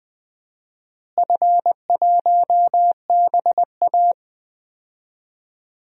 Morse code, heard as a keyed tone, about 20 wpm, 700 Hz. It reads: F1BA
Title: DXCC-100 Call Signs - 4 Characters - Repeated 2x - 20wpm